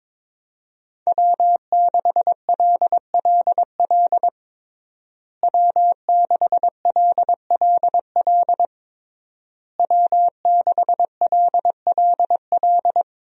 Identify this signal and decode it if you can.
Morse code, heard as a keyed tone, about 22 wpm, 700 Hz. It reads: W6LLL W6LLL W6LLL
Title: USA Call Signs - 5 Characters - Repeated 3x - 22wpm